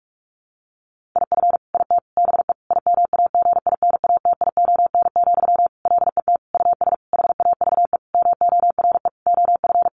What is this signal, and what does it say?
Morse code, heard as a keyed tone, about 44 wpm, 700 Hz. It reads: IF IT BE IMAGINATION, LET US HAVE MORE OF
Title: Sentences from Top 1300 Words 45wpm